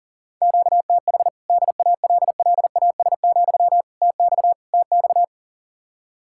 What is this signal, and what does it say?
Morse code, heard as a keyed tone, about 40 wpm, 700 Hz. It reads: QTH DALLAS, TX TX
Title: Practice QSO - Exchange 1 - Beginning - 40wpm